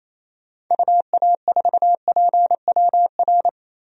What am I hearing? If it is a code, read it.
Morse code, heard as a keyed tone, about 28 wpm, 700 Hz. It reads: UA4PWR
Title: DXCC-100 Call Signs - 6 Characters - 28wpm